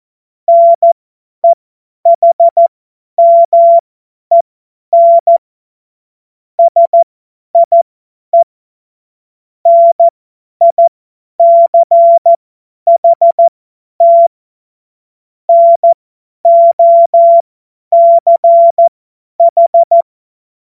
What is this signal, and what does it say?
Morse code, heard as a keyed tone, about 14 wpm, spaced as for 10 wpm, 680 Hz. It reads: NEHMEN SIE NICHT NOCH
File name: Morsekurs / Witze / Witz 25. Januar 2022, 50 F70